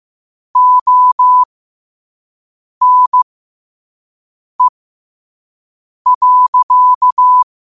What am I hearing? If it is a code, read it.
Morse code, heard as a keyed tone, about 15 wpm, spaced as for 5 wpm, 1000 Hz. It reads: ONE.
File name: qotdfiles / 5wpm